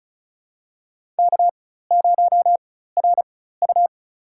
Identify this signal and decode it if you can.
Morse code, heard as a keyed tone, about 36 wpm, spaced as for 16 wpm, 700 Hz. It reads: K0RU
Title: USA Call Signs - 4x Character Spacing - 35wpm